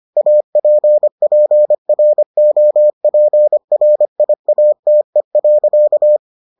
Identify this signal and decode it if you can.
Morse code, heard as a keyed tone, about 25 wpm, 600 Hz. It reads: APPROPRIATE.